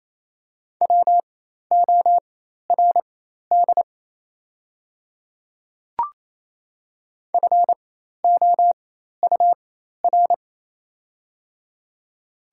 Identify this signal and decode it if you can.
Morse code, heard as a keyed tone, about 28 wpm, spaced as for 13 wpm, 700 Hz. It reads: WORD FOUR
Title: 4x Character Spacing - Top 300 Words - 28wpm